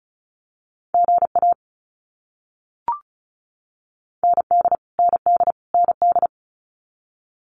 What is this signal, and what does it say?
Morse code, heard as a keyed tone, about 34 wpm, 700 Hz. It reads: GA ND ND ND
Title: US States Repeated 3x 35wpm